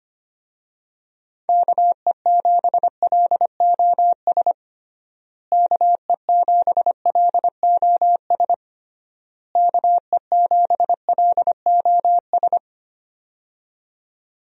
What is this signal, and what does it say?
Morse code, heard as a keyed tone, about 25 wpm, 700 Hz. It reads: KE7LOS KE7LOS KE7LOS
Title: USA Call Signs - 6 Characters - Repeated 3x - 25wpm